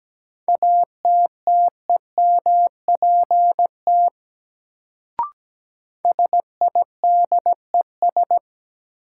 Morse code, heard as a keyed tone, about 17 wpm, 700 Hz. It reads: ATTEMPT SIDES
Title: Sets of 4 Words - New Word Drill - 17wpm